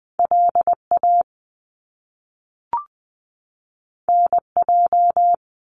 Morse code, heard as a keyed tone, about 20 wpm, 700 Hz. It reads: LA NJ
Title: US States 20wpm